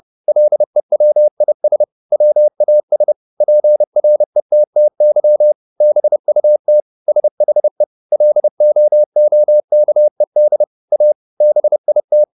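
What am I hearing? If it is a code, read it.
Morse code, heard as a keyed tone, about 30 wpm, 600 Hz. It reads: LEWIS WAS PRETTY BUT SHE LOOKED A BIT
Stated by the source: Morse code at 30 wpm, the novel Never by Ken Follett in English, from CW med Gnister.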